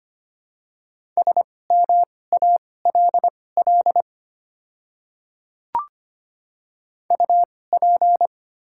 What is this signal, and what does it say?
Morse code, heard as a keyed tone, about 25 wpm, spaced as for 18 wpm, 700 Hz. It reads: SMALL UP
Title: 2x Character Spacing - Top 200 Words - 25wpm